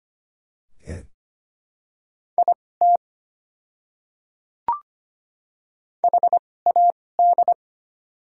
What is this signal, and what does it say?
Morse code, heard as a keyed tone, about 25 wpm, spaced as for 18 wpm, 700 Hz. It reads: IT HAD
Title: 2x Character Spacing - Top 200 Words - 25wpm